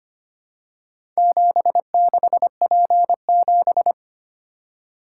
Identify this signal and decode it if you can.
Morse code, heard as a keyed tone, about 25 wpm, 700 Hz. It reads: 76P7
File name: practice / 25wpm / Four Letter-Number Combinations - 25wpm